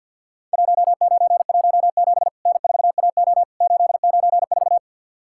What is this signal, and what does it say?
Morse code, heard as a keyed tone, about 46 wpm, 700 Hz. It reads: 1917 N4AO 894
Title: Worked All Europe DX Contest - QTC Exchange - 50wpm